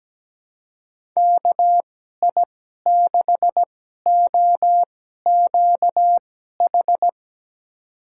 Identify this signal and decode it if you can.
Morse code, heard as a keyed tone, about 17 wpm, spaced as for 12 wpm, 700 Hz. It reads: KI6OQH
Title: USA Call Signs - 6 Characters - 2x Character Spacing - 17wpm